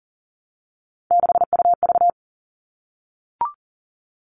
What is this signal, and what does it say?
Morse code, heard as a keyed tone, about 40 wpm, 700 Hz. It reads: 6UV
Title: All Cut Numbers - 3-Digit 40wpm